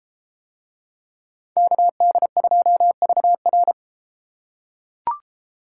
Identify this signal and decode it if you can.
Morse code, heard as a keyed tone, about 31 wpm, 700 Hz. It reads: KD2VR